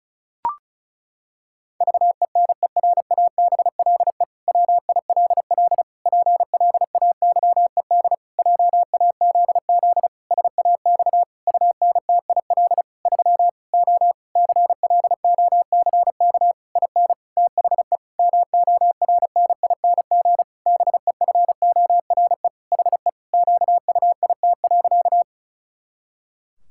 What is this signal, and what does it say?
Morse code, heard as a keyed tone, about 36 wpm, 700 Hz. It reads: VENERABLE WILL PLAYED JAZZ SAX UNTIL 3 O CLOCK IN THE MORNING BEFORE HE QUIT.
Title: Pangram Challenge v2 - 35wpm